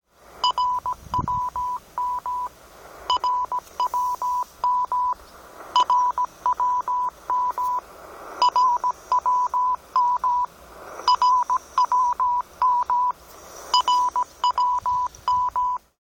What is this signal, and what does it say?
Morse code, heard as a keyed tone, about 17 wpm, 990 Hz. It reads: RWM RWM RWM RWM RWM RWM
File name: RWM2